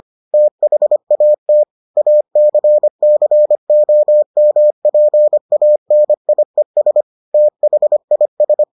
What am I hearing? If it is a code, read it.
Morse code, heard as a keyed tone, about 25 wpm, 600 Hz. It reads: THAT ACCOMPANIES THIS